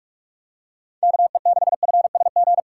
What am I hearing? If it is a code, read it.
Morse code, heard as a keyed tone, about 43 wpm, 700 Hz. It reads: KE6FSG
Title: USA Call Signs - 6 Characters - 6x Character Spacing - 45wpm